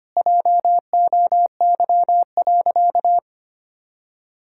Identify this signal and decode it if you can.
Morse code, heard as a keyed tone, about 25 wpm, 700 Hz. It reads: JOY.
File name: practice / 25wpm / Sentences with 2-Word Phrases - 25wpm